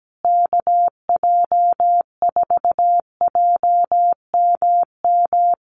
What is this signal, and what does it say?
Morse code, heard as a keyed tone, about 17 wpm, 700 Hz. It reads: KJ4JMM
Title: USA Call Signs Repeated 3x 17wpm